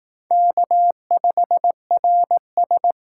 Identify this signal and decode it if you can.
Morse code, heard as a keyed tone, about 18 wpm, 700 Hz. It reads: K5RS
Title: Speed Racing - USA Call Signs 15wpm